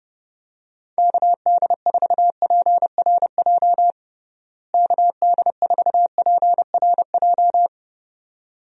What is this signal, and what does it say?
Morse code, heard as a keyed tone, about 30 wpm, 700 Hz. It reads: KD4PRJ KD4PRJ
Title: USA Call Signs - 6 Characters - Repeated 2x - 30wpm